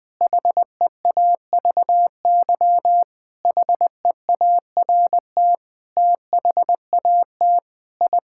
Morse code, heard as a keyed tone, about 20 wpm, 700 Hz. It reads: HEAVY HEART THAT I
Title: Sentences with 7-Word Phrases - Spoken Beginning - 20wpm